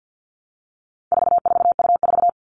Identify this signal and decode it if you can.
Morse code, heard as a keyed tone, about 50 wpm, 700 Hz. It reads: 44U4
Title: All Cut Numbers - 4-Digit 50wpm